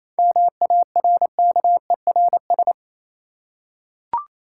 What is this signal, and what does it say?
Morse code, heard as a keyed tone, about 27 wpm, 690 Hz. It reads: MARKERS
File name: Easy Words - Speed-Racing with Hints - 22wpm